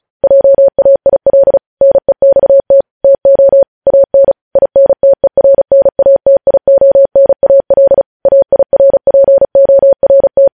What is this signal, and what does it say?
Morse code, heard as a keyed tone, about 36 wpm, 550 Hz. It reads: JAIL NEXT TO AN INTERNATIONAL AIRPORT
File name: podcast / News headlines-1x-35x35